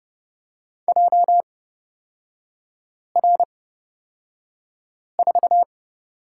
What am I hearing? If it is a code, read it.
Morse code, heard as a keyed tone, about 30 wpm, spaced as for 5 wpm, 700 Hz. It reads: JR4
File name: Lesson 38e - Letter q - Three Characters 30wpm